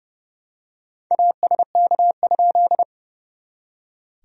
Morse code, heard as a keyed tone, about 30 wpm, 660 Hz. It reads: ASK?
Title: SOTA QSO Practice 30wpm